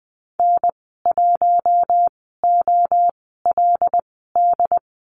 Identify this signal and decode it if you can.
Morse code, heard as a keyed tone, about 20 wpm, spaced as for 14 wpm, 700 Hz. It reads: N1OLD
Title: 082 Lesson 14g - Number 2 - Call Signs 20wpm